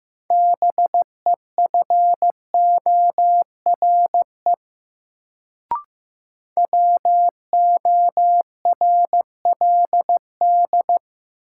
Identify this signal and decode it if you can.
Morse code, heard as a keyed tone, about 15 wpm, 700 Hz. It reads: BEFORE WORLD
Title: Sentences from Top 200 Words - New Word Drill - 15wpm